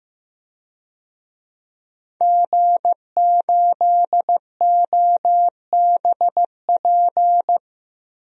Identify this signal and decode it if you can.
Morse code, heard as a keyed tone, about 15 wpm, 700 Hz. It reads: G8OBP
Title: DXCC-100 Call Signs - 5 Characters - 15wpm